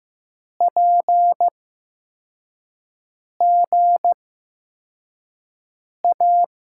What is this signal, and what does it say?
Morse code, heard as a keyed tone, about 15 wpm, spaced as for 4 wpm, 700 Hz. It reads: PGA
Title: USA Call Signs - 5 Characters - 8x Character Spacing - 15wpm